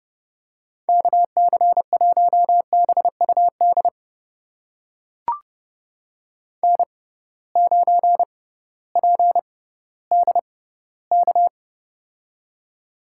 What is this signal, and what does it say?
Morse code, heard as a keyed tone, about 30 wpm, spaced as for 16 wpm, 700 Hz. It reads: KC1BUD N9PDK
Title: USA Call Signs - 6x Character Spacing - 30wpm